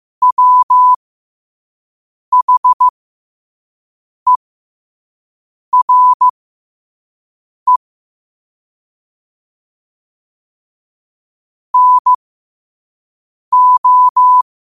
Morse code, heard as a keyed tone, about 15 wpm, spaced as for 5 wpm, 1000 Hz. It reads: WHERE NO
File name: qotdfiles / 5wpm